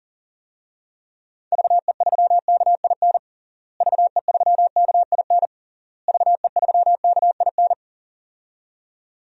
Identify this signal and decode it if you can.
Morse code, heard as a keyed tone, about 40 wpm, 700 Hz. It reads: VE3KIN VE3KIN VE3KIN
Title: DXCC-100 Call Signs - 6 Characters - Repeated 3x - 40wpm